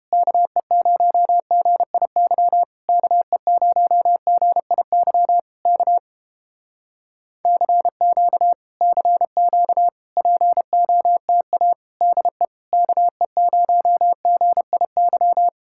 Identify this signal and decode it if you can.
Morse code, heard as a keyed tone, about 31 wpm, 700 Hz. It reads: KE0GIY KE0GIY K CQ CQ POTA DE KE0GIY